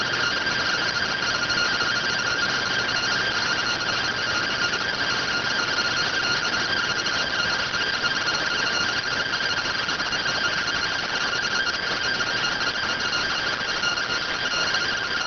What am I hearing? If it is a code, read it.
Two-tone teleprinter signal, about 170 Hz shift.